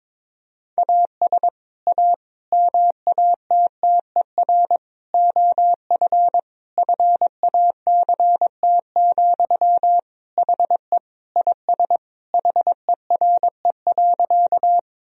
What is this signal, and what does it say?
Morse code, heard as a keyed tone, about 22 wpm, 700 Hz. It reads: AS A MATTER OF FACT, HE IS HERE.
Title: Sentences from Top 1000 Words - Review Part 3 - 22wpm